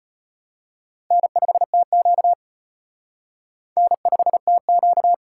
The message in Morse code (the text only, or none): N5TQ N5TQ